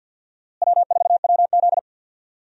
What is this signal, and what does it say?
Morse code, heard as a keyed tone, about 48 wpm, 700 Hz. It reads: W4WZ
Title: USA Call Signs - 4 Characters - Repeated 3x - 50wpm